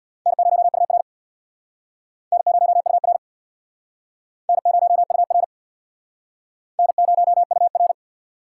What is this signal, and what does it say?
Morse code, heard as a keyed tone, about 63 wpm, 700 Hz. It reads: N0UR N0UR N0UR N0UR